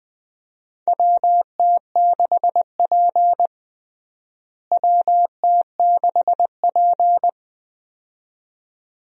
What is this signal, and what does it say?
Morse code, heard as a keyed tone, about 20 wpm, 700 Hz. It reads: WT6P WT6P
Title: USA Call Signs - 4 Characters - Repeated 2x - 20wpm